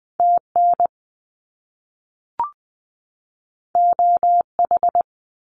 Morse code, heard as a keyed tone, about 20 wpm, 700 Hz. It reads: TN OH
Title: US States 20wpm